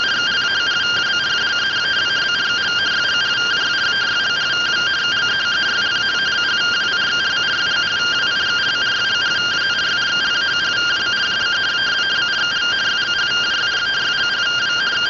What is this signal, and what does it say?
Two-tone teleprinter signal, about 170 Hz shift.